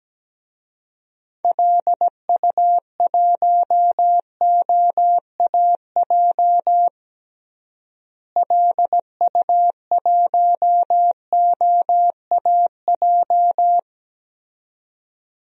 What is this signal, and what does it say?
Morse code, heard as a keyed tone, about 17 wpm, 700 Hz. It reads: LU1OAJ LU1OAJ
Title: DXCC-100 Call Signs - 6 Characters - Repeated 2x - 17wpm